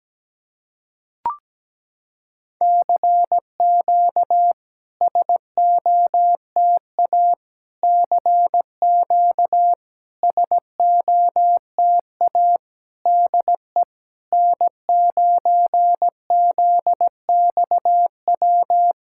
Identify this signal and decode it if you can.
Morse code, heard as a keyed tone, about 17 wpm, 700 Hz. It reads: CQ SOTA CQ SOTA DE N9ZXW
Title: SOTA QSO Practice 17wpm